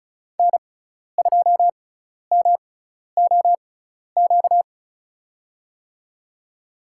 Morse code, heard as a keyed tone, about 36 wpm, spaced as for 12 wpm, 700 Hz. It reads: N2MOQ
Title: USA Call Signs - 5 Characters - 6x Character Spacing - 35wpm